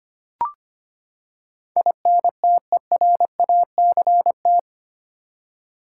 Morse code, heard as a keyed tone, about 25 wpm, 700 Hz. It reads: INTERACT
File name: Eight Letter Words - 25wpm